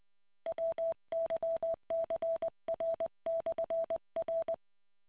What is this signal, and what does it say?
Morse code, heard as a keyed tone, about 20 wpm, 660 Hz. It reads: WYCR/R